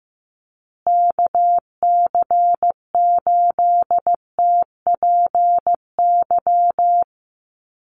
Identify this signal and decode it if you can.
Morse code, heard as a keyed tone, about 15 wpm, 700 Hz. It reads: KC8TPY
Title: USA Call Signs Repeated 3x 15wpm